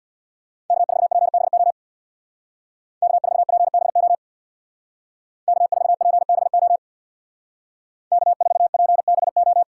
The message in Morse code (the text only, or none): K4PBO K4PBO K4PBO K4PBO